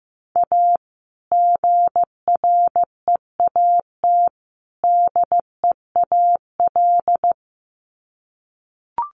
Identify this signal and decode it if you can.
Morse code, heard as a keyed tone, about 15 wpm, 700 Hz. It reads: A GREAT DEAL
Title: Sets of 3 Words 15wpm